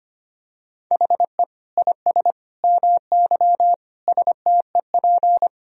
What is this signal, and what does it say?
Morse code, heard as a keyed tone, about 25 wpm, 700 Hz. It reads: HE IS MY STEP
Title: Sentences from Top 1000 Words - Review Part 2 - 25wpm